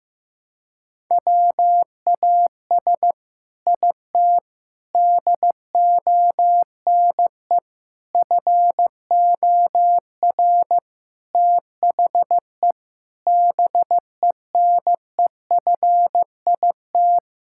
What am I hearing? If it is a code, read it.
Morse code, heard as a keyed tone, about 15 wpm, 700 Hz. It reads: WAS IT DONE FOR THE BENEFIT
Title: Sentences from Top 1000 Words - Review Part 6 - 15wpm